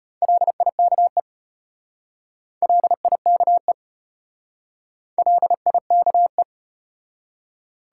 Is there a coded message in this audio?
Yes: Morse code from 0.2 to 6.4 s, heard as a keyed tone, about 33 wpm, 700 Hz. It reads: LIKE LIKE LIKE